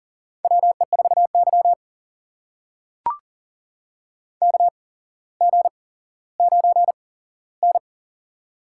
Morse code, heard as a keyed tone, about 44 wpm, spaced as for 17 wpm, 700 Hz. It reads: WE4Y KG9N
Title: USA Call Signs - 4 Characters - 8x Character Spacing - 40wpm